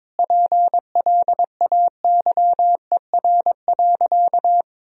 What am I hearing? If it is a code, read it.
Morse code, heard as a keyed tone, about 22 wpm, 700 Hz. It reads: PLAYER.
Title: Sentences from Top 1000 Words - Review Part 4 - 22wpm